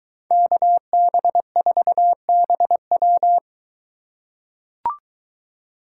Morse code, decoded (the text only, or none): KB4BW